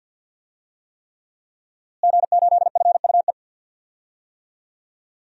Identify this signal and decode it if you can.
Morse code, heard as a keyed tone, about 50 wpm, 700 Hz. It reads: G8FUE
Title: DXCC-100 Call Signs - 5 Characters - Repeated 3x - 50wpm